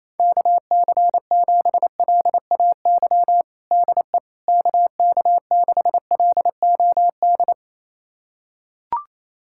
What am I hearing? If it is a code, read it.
Morse code, heard as a keyed tone, about 28 wpm, 700 Hz. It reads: KC7LAY DE KK6LOD
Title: USA Callsign DE Callsign - 6 Characters - 28wpm